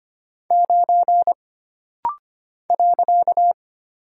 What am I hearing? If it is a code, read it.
Morse code, heard as a keyed tone, about 25 wpm, 700 Hz. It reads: 9 .
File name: Single Character (Letter:Number:Punctuation:Prosign) - Rapid-Fire - 25wpm